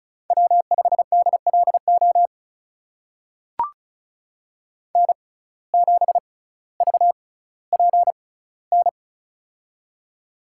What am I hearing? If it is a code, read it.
Morse code, heard as a keyed tone, about 36 wpm, spaced as for 18 wpm, 700 Hz. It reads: W5DLO N7VPN
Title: USA Call Signs - 5 Characters - 6x Character Spacing - 35wpm